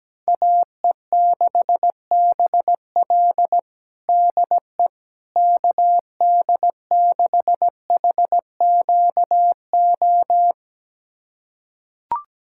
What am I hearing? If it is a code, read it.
Morse code, heard as a keyed tone, about 17 wpm, 700 Hz. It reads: AE6BL DE KD6HQO
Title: USA Callsign DE Callsign 17wpm